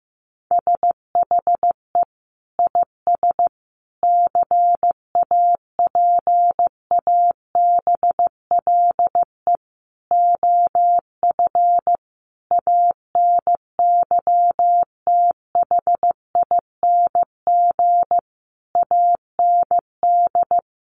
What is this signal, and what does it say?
Morse code, heard as a keyed tone, about 15 wpm, 700 Hz. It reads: SHE IS CAPABLE OF ANYTHING AND
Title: Sentences from Top 1200 Words 15wpm